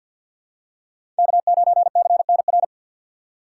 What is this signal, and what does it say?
Morse code, heard as a keyed tone, about 53 wpm, 700 Hz. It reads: K9CNR
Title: USA Call Signs - 5 Characters - 8x Character Spacing - 50wpm